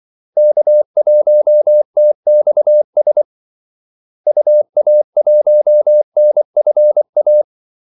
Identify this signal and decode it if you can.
Morse code, heard as a keyed tone, about 24 wpm, 600 Hz. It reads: K1TXS UA1NFA
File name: Callsigns-24wpm